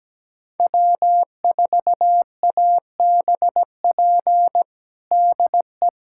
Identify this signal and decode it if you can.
Morse code, heard as a keyed tone, about 17 wpm, 700 Hz. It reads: W4ABP DE
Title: USA Callsign DE Callsign 17wpm